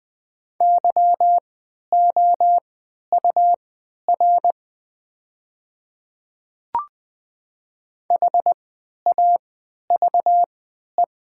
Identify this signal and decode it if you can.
Morse code, heard as a keyed tone, about 20 wpm, spaced as for 11 wpm, 700 Hz. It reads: YOUR HAVE
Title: 3x Character Spacing - Top 100 Words - 20wpm